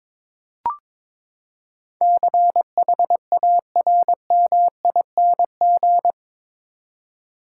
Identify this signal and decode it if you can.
Morse code, heard as a keyed tone, about 22 wpm, 700 Hz. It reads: CHARMING
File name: practice / 22wpm / Eight Letter Words - 22wpm